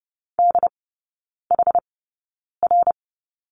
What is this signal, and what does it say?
Morse code, heard as a keyed tone, about 30 wpm, spaced as for 9 wpm, 700 Hz. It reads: DHR
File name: Lesson 17e - Letter u - Three Characters 30wpm